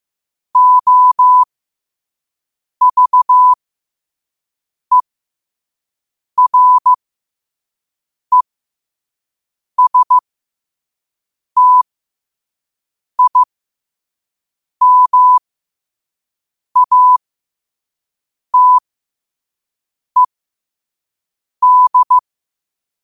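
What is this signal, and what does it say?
Morse code, heard as a keyed tone, about 15 wpm, spaced as for 5 wpm, 1000 Hz. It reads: OVERESTIMATED
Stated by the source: Morse code quote of the day at 5 wpm.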